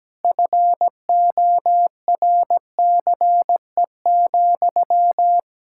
Morse code, heard as a keyed tone, about 17 wpm, 700 Hz. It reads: FORCE,
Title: Sentences from Top 1000 Words - Review Part 5 - 17wpm